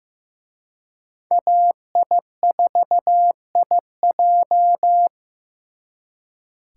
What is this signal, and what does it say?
Morse code, heard as a keyed tone, about 18 wpm, 700 Hz. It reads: AI4IJ
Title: Speed Racing - USA 2x2 Call Signs 15wpm